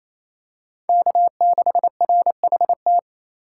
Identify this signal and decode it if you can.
Morse code, heard as a keyed tone, about 28 wpm, 700 Hz. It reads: K6RHT
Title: USA Call Signs - 4x Character Spacing - 28wpm